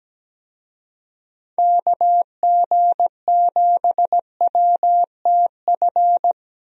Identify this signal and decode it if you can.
Morse code, heard as a keyed tone, about 17 wpm, 700 Hz. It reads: KG7WTF
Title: USA Call Signs - 6 Characters - Repeated 2x - 17wpm